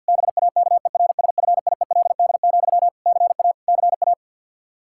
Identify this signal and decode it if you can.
Morse code, heard as a keyed tone, about 51 wpm, 700 Hz. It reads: BAKERSFIELD, CA CA
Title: Practice QSO - Exchange 1 - Beginning - 50wpm